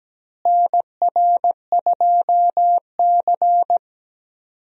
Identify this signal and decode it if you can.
Morse code, heard as a keyed tone, about 17 wpm, 700 Hz. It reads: NR2C
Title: USA Call Signs - 4 Characters - Repeated 3x - 17wpm